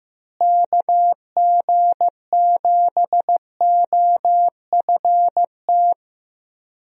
Morse code, heard as a keyed tone, about 15 wpm, 700 Hz. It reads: KG7OFT
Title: USA Call Signs - 6 Characters - Repeated 3x - 15wpm